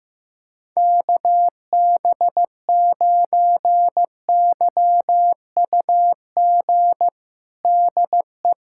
Morse code, heard as a keyed tone, about 15 wpm, 700 Hz. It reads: KB9YUG DE
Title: USA Callsign DE Callsign - 6 Characters - 15wpm